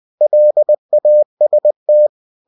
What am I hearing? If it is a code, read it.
Morse code, heard as a keyed tone, about 20 wpm, 600 Hz. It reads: LAST
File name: cw-books / winnie-the-pooh / winnie-20wpm-page-0076